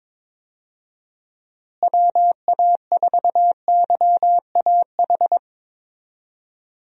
Morse code, heard as a keyed tone, about 22 wpm, 700 Hz. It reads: WA4YAH
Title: USA Call Signs - 6 Characters - Repeated 2x - 22wpm